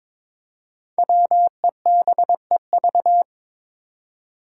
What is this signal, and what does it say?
Morse code, heard as a keyed tone, about 22 wpm, 700 Hz. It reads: WEBEV
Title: Five Challenging Letter-Number Combinations 22wpm